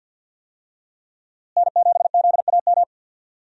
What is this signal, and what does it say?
Morse code, heard as a keyed tone, about 49 wpm, 700 Hz. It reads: N7ZAM
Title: USA Call Signs - 5 Characters - Repeated 3x - 50wpm